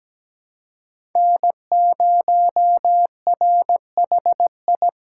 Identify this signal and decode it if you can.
Morse code, heard as a keyed tone, about 17 wpm, 700 Hz. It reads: N0RHI
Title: USA Call Signs - 5 Characters - 4x Character Spacing - 17wpm